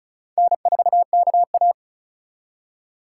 Morse code, heard as a keyed tone, about 43 wpm, 700 Hz. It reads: N4KA
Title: Speed Racing - USA Call Signs 35wpm